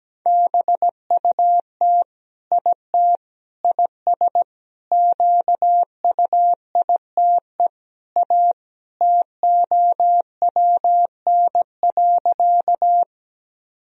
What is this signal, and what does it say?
Morse code, heard as a keyed tone, about 17 wpm, 700 Hz. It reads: BUT IT IS QUITE A TOWN.
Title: Sentences from Top 1000 Words - Review Part 2 - 17wpm